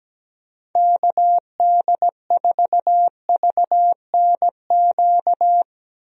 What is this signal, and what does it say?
Morse code, heard as a keyed tone, about 17 wpm, 700 Hz. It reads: KD4VNQ
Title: USA Call Signs - 6 Characters - 17wpm